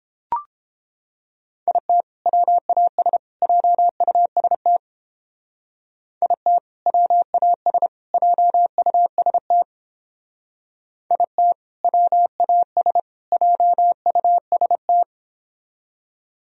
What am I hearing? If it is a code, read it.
Morse code, heard as a keyed tone, about 29 wpm, 700 Hz. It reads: IT WAS JUST IT WAS JUST IT WAS JUST